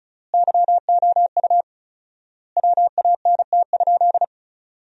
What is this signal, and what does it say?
Morse code, heard as a keyed tone, about 34 wpm, 700 Hz. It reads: YOU WANT?
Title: 4x Word-Spacing - Sentences from Top 200 Words - 35wpm